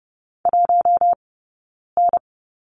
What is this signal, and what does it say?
Morse code, heard as a keyed tone, about 30 wpm, 700 Hz. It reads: 1 N
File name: Lesson 19d - Letter w - Two Characters 30wpm